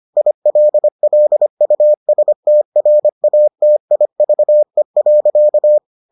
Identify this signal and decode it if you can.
Morse code, heard as a keyed tone, about 25 wpm, 600 Hz. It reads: ILLUSTRATIVE.